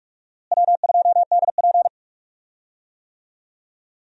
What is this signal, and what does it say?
Morse code, heard as a keyed tone, about 46 wpm, 700 Hz. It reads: W2DP
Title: USA Call Signs - 4 Characters - Repeated 3x - 45wpm